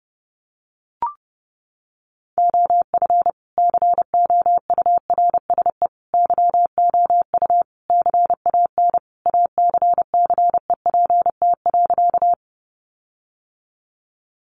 Morse code, heard as a keyed tone, about 28 wpm, 700 Hz. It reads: OF COURSE YOU CAN ACCEPT.
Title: Sentences from Top 1300 Words 30wpm